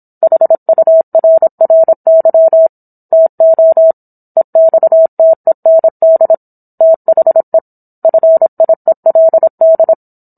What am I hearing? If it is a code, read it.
Morse code, heard as a keyed tone, about 27 wpm, 650 Hz. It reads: HURRY TO EXTEND THE FIELD
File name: Book One / 17-Chapter 17 - 26 WPM